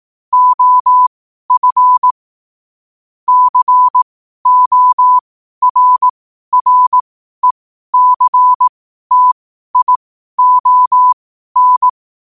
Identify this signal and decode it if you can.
Morse code, heard as a keyed tone, about 18 wpm, spaced as for 12 wpm, 1000 Hz. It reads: OF CORRECTION
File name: qotdfiles / 12wpm